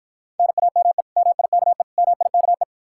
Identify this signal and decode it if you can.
Morse code, heard as a keyed tone, about 54 wpm, 700 Hz. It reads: NAME MIKE MIKE